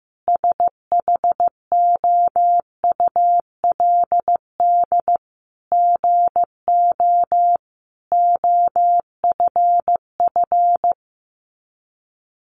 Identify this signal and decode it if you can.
Morse code, heard as a keyed tone, about 14 wpm, 700 Hz. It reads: SHOULD GO OFF
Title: A Sound Of Thunder 15wpm